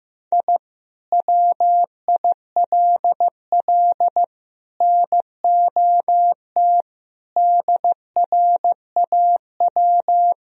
Morse code, heard as a keyed tone, about 15 wpm, 700 Hz. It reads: I WILL NOT DRAW
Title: Sentences from Top 1000 Words - Review Part 1 - 15wpm